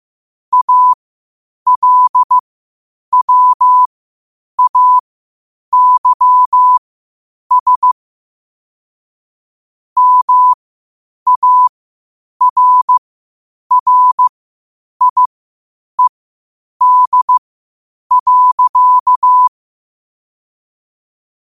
Morse code, heard as a keyed tone, about 15 wpm, spaced as for 8 wpm, 1000 Hz. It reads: ALWAYS MARRIED.
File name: qotdfiles / 8wpm